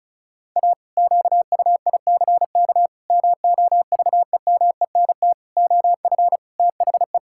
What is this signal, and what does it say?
Morse code, heard as a keyed tone, about 36 wpm, 700 Hz. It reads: A QUICK MOVEMENT OF THE
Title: Pangram Challenge v2 - 35wpm